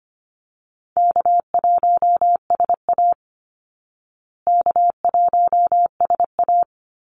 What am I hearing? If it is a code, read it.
Morse code, heard as a keyed tone, about 25 wpm, 700 Hz. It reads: K1SA K1SA
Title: Easy Call Signs Repeated 2x - 25wpm